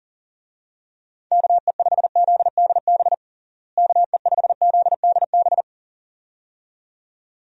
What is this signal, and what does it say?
Morse code, heard as a keyed tone, about 40 wpm, 700 Hz. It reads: KE5ZDB KE5ZDB
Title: USA Call Signs - 6 Characters - Repeated 2x - 40wpm